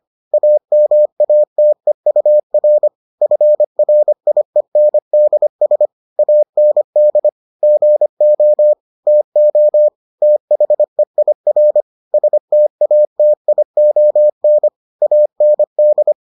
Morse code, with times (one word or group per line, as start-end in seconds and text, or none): 0.3-2.9 AMATEUR
3.2-5.9 FRIENDS
6.2-7.3 AND
7.6-8.7 GO
9.1-9.9 TO
10.2-11.8 THEIR
12.1-14.7 STATION
15.0-16.1 AND